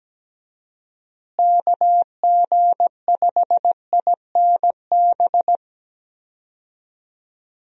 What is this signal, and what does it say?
Morse code, heard as a keyed tone, about 17 wpm, 700 Hz. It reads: KG5INB
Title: USA Call Signs - 6 Characters - Repeated 2x - 17wpm